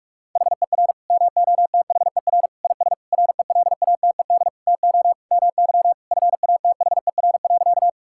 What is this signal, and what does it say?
Morse code, heard as a keyed tone, about 46 wpm, 700 Hz. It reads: HER MOTHER IS RELATED TO MY FATHER.
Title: Sentences from Top 1000 Words - Review Part 5 - 45wpm